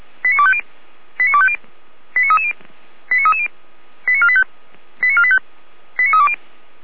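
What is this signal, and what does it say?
Five-tone selective call (ZVEI), heard as quick runs of five tones, tones of about 70 ms. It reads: 89250 89250 89330 89330 89485 89485 89320
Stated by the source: ZVEI five-tone selective call received with an SDR, from the audio archive of Feuerwehr Orlamünde (fire brigade).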